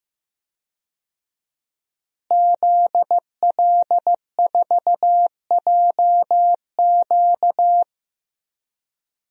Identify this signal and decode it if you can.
Morse code, heard as a keyed tone, about 15 wpm, 700 Hz. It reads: ZL4JQ
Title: DXCC-100 Call Signs - 5 Characters - Repeated 3x - 15wpm